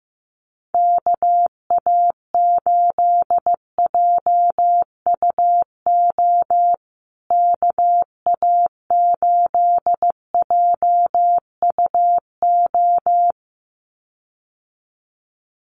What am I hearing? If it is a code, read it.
Morse code, heard as a keyed tone, about 15 wpm, 700 Hz. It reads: KA8JUO KA8JUO
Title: USA Call Signs Repeated 2x 15wpm